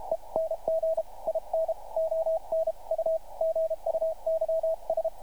Morse code, heard as a keyed tone, about 32 wpm, 650 Hz, with radio noise. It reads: ENG IN ON U GUYS
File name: archived / CRSnet